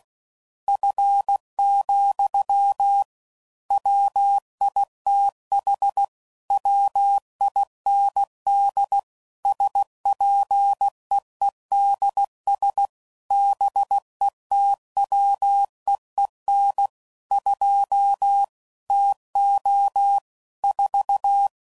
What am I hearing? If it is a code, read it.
Morse code, heard as a keyed tone, about 16 wpm, 800 Hz. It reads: F, WITH WIND SPEEDS BETWEEN 2 TO 4
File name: CW Code Groups @15WPM / Weather Sets / Weather 4 14